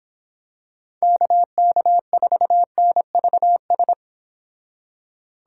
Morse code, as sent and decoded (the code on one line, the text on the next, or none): -.- -.- ....- -. ...- ...
KK4NVS